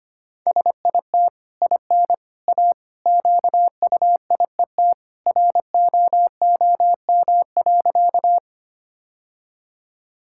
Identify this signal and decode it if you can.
Morse code, heard as a keyed tone, about 25 wpm, 690 Hz. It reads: SIT IN A QUIET ROOM.
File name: Easy Sentences with Hints - 25wpm